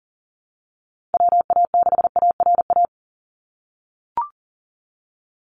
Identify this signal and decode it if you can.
Morse code, heard as a keyed tone, about 39 wpm, 700 Hz. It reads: WA6ARA
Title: NA SOTA Chaser Callsigns 40wpm